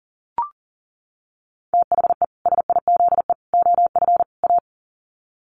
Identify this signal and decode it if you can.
Morse code, heard as a keyed tone, about 38 wpm, 700 Hz. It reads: THE SIZE OF A
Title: Sets of 4 Words 40wpm